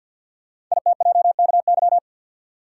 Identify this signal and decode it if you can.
Morse code, heard as a keyed tone, about 48 wpm, 700 Hz. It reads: ITJKY
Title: Code Groups - 5 Letters - 50wpm